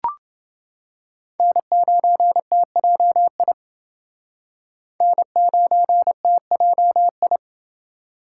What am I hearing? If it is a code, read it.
Morse code, heard as a keyed tone, about 28 wpm, 700 Hz. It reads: N9TJI N9TJI